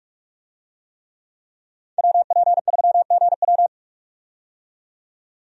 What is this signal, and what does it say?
Morse code, heard as a keyed tone, about 46 wpm, 700 Hz. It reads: WP3GW
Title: USA Call Signs - 5 Characters - 45wpm